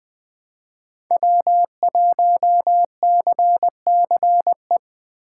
Morse code, heard as a keyed tone, about 20 wpm, 700 Hz. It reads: W1CCE
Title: K1USN SST QSO Practice - 20wpm